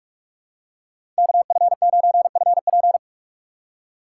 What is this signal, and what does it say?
Morse code, heard as a keyed tone, about 43 wpm, 700 Hz. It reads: KF9FP
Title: USA Call Signs - 5 Characters - 4x Character Spacing - 45wpm